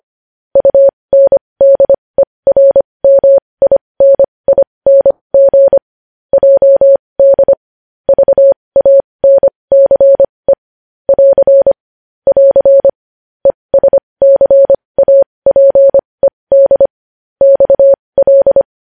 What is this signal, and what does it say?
Morse code, heard as a keyed tone, about 25 wpm, spaced as for 20 wpm, 550 Hz. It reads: UNDERMINING JD VANCE <AR> <AR> ESCAPED XL
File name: podcast / News headlines-1x-25x20